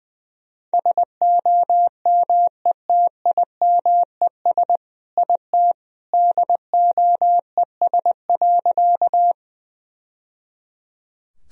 Morse code, heard as a keyed tone, about 20 wpm, 700 Hz. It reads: SOMETIMES IT DOES.
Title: Sentences from Top 1000 Words - Review Part 3 - 20wpm